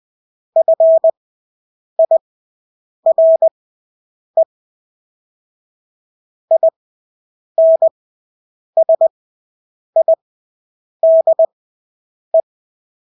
Morse code, heard as a keyed tone, about 20 wpm, spaced as for 8 wpm, 650 Hz. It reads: FIRE INSIDE
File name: Engelske ord 20/8 wpm